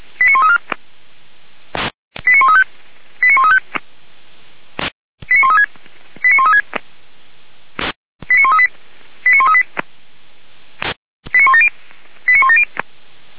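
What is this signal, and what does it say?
Five-tone selective call (ZVEI), heard as quick runs of five tones, tones of about 80 ms. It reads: 89134 89135 89135 89136 89136 89138 89138 89170 89170
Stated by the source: Five-tone ZVEI selective call received on SDR, fire brigade Orlamünde (feuerwehr-orlamuende).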